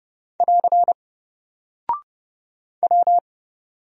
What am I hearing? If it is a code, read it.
Morse code, heard as a keyed tone, about 30 wpm, 700 Hz. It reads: <AR> W
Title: Single Character (Letter:Number:Punctuation:Prosign) 30wpm